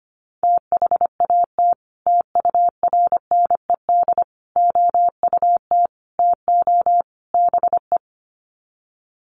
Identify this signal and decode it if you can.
Morse code, heard as a keyed tone, about 25 wpm, 700 Hz. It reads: THAT TURNED OUT TO BE
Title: Sets of 5 Words Encore2 25wpm